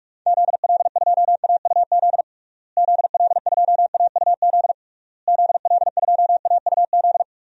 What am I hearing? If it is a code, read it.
Morse code, heard as a keyed tone, about 46 wpm, 700 Hz. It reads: ZL2AUZ ZL2AUZ ZL2AUZ
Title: DXCC-100 Call Signs - 6 Characters - Repeated 3x - 45wpm